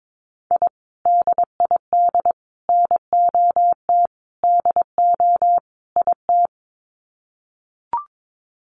Morse code, heard as a keyed tone, about 22 wpm, 700 Hz. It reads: I DID NOT DO IT
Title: Sets of 5 Words Encore2 22wpm